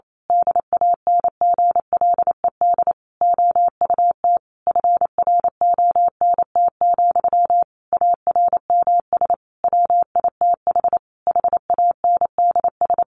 Morse code, heard as a keyed tone, about 28 wpm, 700 Hz. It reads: DANGLED OUT FRONT, ARMS WITH HANDS
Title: A Sound Of Thunder 28wpm